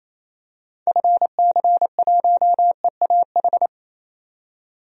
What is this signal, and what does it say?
Morse code, heard as a keyed tone, about 28 wpm, 700 Hz. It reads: FC1EAH
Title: DXCC-100 Call Signs - 6 Characters - 28wpm